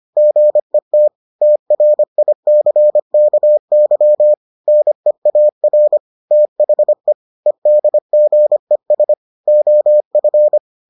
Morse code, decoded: GET TRICKY NEAR THE EDGES OF

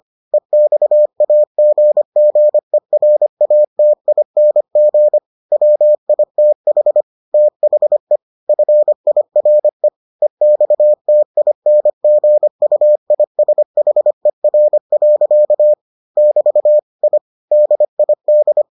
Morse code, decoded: EXAGGERATING WITH THE FIRE EXTINGUISHER. <BT> I DID

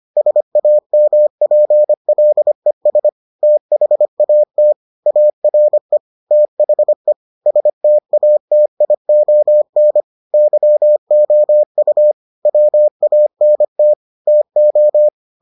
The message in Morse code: SAMPLES THAT ARE THE STATION YOU WANT TO

